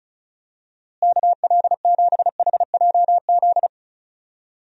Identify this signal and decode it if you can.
Morse code, heard as a keyed tone, about 34 wpm, 700 Hz. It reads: KL7HJZ
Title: DXCC-100 Call Signs - 6 Characters - Repeated 2x - 35wpm